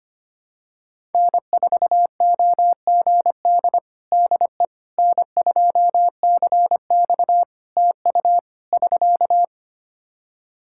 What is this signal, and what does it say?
Morse code, heard as a keyed tone, about 25 wpm, 700 Hz. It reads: N4OGD DE N2CX TU <SK>
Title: Practice QSO - Exchange 4 - Ending - 25wpm